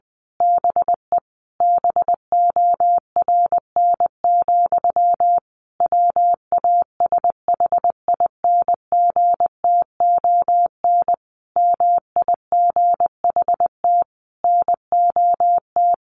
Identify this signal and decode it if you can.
Morse code, heard as a keyed tone, about 20 wpm, 700 Hz. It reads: BE BORN, WASHINGTON MIGHT NOT
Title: A Sound Of Thunder 20wpm